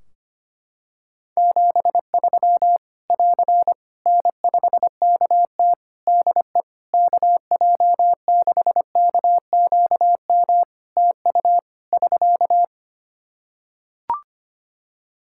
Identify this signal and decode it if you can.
Morse code, heard as a keyed tone, about 25 wpm, 700 Hz. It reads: 73 <AR> N5KT DE KJ6KQM TU <SK>
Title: Practice QSO - Exchange 4 - Ending - 25wpm